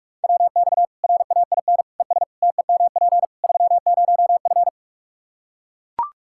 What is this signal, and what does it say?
Morse code, heard as a keyed tone, about 43 wpm, 700 Hz. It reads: WX RAIN ES TEMP 30F
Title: K1USN SST QSO Practice 45wpm